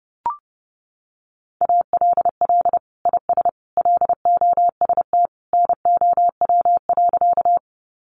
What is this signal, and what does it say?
Morse code, heard as a keyed tone, about 30 wpm, 700 Hz. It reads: ALL IS LOST NOW.
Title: Sentences from Top 1400 Words 30wpm